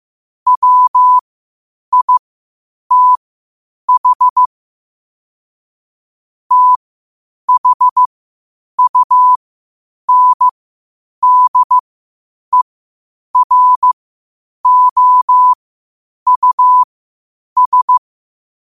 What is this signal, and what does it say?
Morse code, heard as a keyed tone, about 15 wpm, spaced as for 8 wpm, 1000 Hz. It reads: WITH THUNDEROUS
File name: qotdfiles / 8wpm